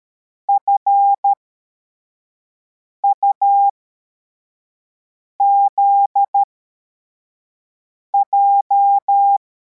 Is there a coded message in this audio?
Yes: Morse code from 0.5 to 9.4 s, heard as a keyed tone, about 13 wpm, spaced as for 4 wpm, 800 Hz. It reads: FUZJ